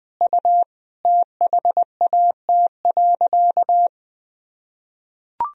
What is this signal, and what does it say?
Morse code, heard as a keyed tone, about 20 wpm, 700 Hz. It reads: U THAT.
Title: Sentences from Top 300 Words and Abbreviations 20wpm